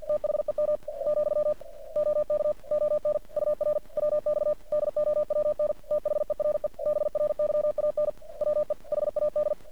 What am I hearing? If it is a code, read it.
Morse code, heard as a keyed tone, about 50 wpm, 600 Hz. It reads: THEM , OK ON UR WX DOWN THERE BRYAN WE HAD
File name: archived / CRSnet